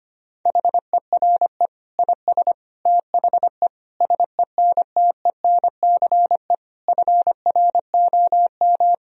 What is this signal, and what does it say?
Morse code, heard as a keyed tone, about 25 wpm, 700 Hz. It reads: HERE IS THE SENTENCE FROM
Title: Sentences from Top 1000 Words - Review Part 1 - 25wpm